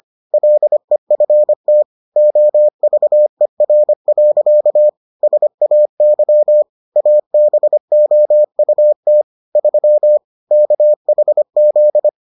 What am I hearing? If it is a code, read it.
Morse code, heard as a keyed tone, about 25 wpm, 600 Hz. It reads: LEFT OVER. SAY ABOUT 3 KHZ